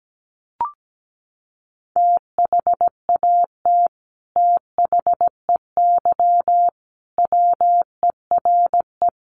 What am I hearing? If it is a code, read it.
Morse code, heard as a keyed tone, about 17 wpm, 690 Hz. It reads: THAT THEY WERE
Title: Sets of 5 Words Encore 17wpm